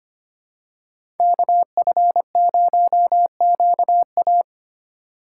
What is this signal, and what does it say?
Morse code, heard as a keyed tone, about 25 wpm, 700 Hz. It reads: KF0QA
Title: USA Call Signs - 5 Characters - 4x Character Spacing - 25wpm